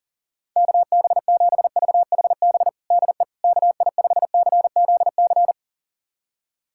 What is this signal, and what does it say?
Morse code, heard as a keyed tone, about 40 wpm, 700 Hz. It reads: KB7VHB DE KI5CZC
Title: USA Callsign DE Callsign - 6 Characters - 40wpm